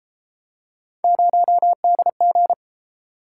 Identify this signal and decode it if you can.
Morse code, heard as a keyed tone, about 33 wpm, 700 Hz. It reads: 0DG